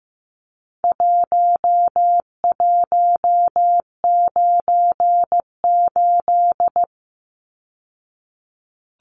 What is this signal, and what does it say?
Morse code, heard as a keyed tone, about 15 wpm, 700 Hz. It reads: 1198
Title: Numbers - Contest Style - 4-Digit 15wpm